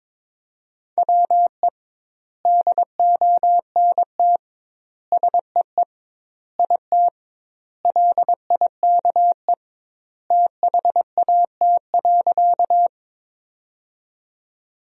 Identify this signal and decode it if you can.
Morse code, heard as a keyed tone, about 22 wpm, 700 Hz. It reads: WE DONT SEE IT LIKE THAT.
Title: 2x Word-Spacing - Sentences from Top 300 Words - 22wpm